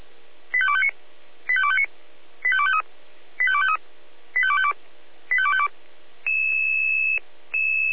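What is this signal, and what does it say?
Five-tone selective call (ZVEI), heard as quick runs of five tones, tones of about 70 ms. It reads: 85259 85259 85253 85253 85252 85252
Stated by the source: Five-tone ZVEI selective call 85280, received by SDR.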